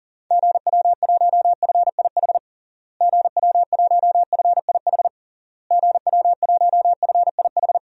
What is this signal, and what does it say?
Morse code, heard as a keyed tone, about 40 wpm, 700 Hz. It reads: GW1FIH GW1FIH GW1FIH
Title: DXCC-100 Call Signs - 6 Characters - Repeated 3x - 40wpm